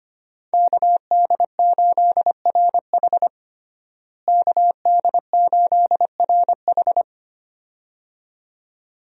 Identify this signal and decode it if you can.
Morse code, heard as a keyed tone, about 25 wpm, 700 Hz. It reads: KD8RH KD8RH
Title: USA Call Signs - 5 Characters - Repeated 2x - 25wpm